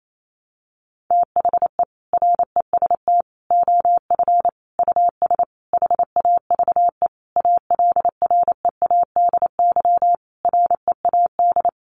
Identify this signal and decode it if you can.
Morse code, heard as a keyed tone, about 28 wpm, 700 Hz. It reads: THE REST OF US HAVE ALREADY READ
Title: Sentences from Top 700 Words 28wpm